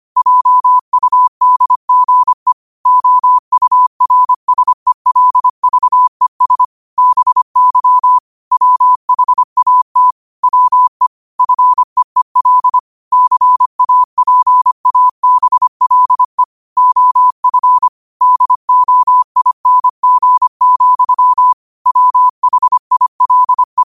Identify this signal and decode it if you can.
Morse code, heard as a keyed tone, about 25 wpm, 1000 Hz. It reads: JUDGE OURSELVES BY WHAT WE FEEL CAPABLE OF DOING, WHILE